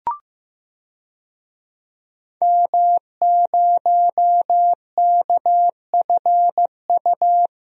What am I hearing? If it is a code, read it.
Morse code, heard as a keyed tone, about 15 wpm, 700 Hz. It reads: M0KFU
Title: DXCC-100 Call Signs - 5 Characters - Repeated 2x - 15wpm